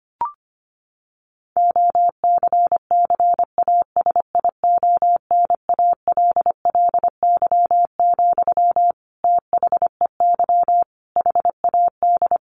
Morse code, heard as a keyed tone, about 25 wpm, 700 Hz. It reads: OCCASIONALLY, THEY HAD
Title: Sentences from Top 1400 Words 25wpm